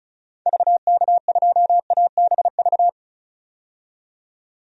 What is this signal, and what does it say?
Morse code, heard as a keyed tone, about 34 wpm, 700 Hz. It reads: VK2ABV
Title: DXCC-100 Call Signs - 6 Characters - 35wpm